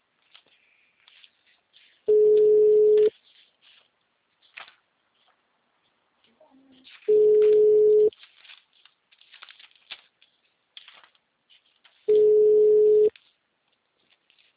Ringback tone (Europe, 425 Hz).
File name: out- -2012- - - .4531